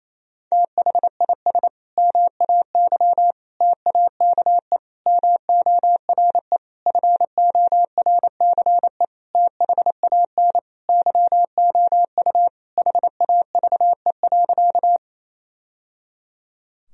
Morse code, heard as a keyed tone, about 28 wpm, 700 Hz. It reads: THIS MAY TAKE MORE FORCE THAN YOU HAVE.
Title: Sentences from Top 1000 Words - Review Part 3 - 28wpm